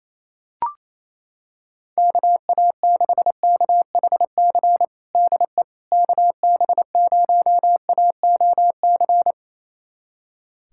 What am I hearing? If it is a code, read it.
Morse code, heard as a keyed tone, about 28 wpm, 700 Hz. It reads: KA6KHC DE KB0AOC
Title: USA Callsign DE Callsign - 6 Characters - 28wpm